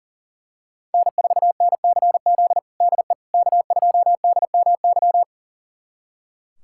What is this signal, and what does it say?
Morse code, heard as a keyed tone, about 40 wpm, 700 Hz. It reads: N4NCZ DE K2DMY
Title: USA Callsign DE Callsign - 5 Characters - 40wpm